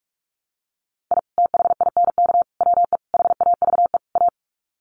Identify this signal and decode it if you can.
Morse code, heard as a keyed tone, about 44 wpm, 700 Hz. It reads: I THINK WE HAVE A
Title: Sets of 5 Words Encore 45wpm